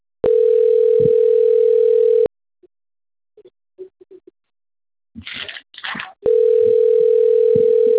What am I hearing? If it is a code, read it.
Ringback tone (North America), 440 + 480 Hz.